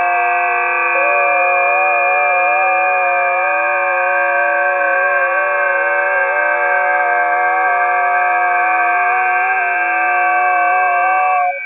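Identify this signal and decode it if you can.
FT8.
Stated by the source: FT8 signals recorded on a busy 20 m band.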